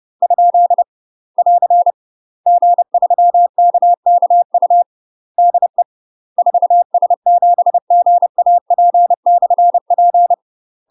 Morse code, heard as a keyed tone, about 30 wpm, 700 Hz. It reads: ? <AR> G3KKU DE 4S7GAP/P
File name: UK MASTER QSOs TRAINING / 30WPM / UK MASTER QSOs 0009